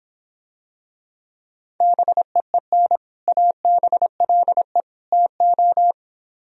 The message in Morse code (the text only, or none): BEEN ABLE TO